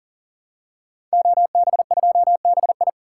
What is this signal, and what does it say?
Morse code, heard as a keyed tone, about 40 wpm, 700 Hz. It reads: OB2BI
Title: Code Groups - 5 Letter:Numbers - 40wpm